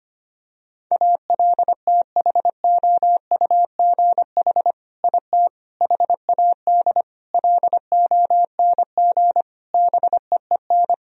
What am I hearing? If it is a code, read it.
Morse code, heard as a keyed tone, about 25 wpm, 700 Hz. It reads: ALTHOUGH IT HAD LONG BEEN
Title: Sentences from Top 1000 Words - Review Part 3 - 25wpm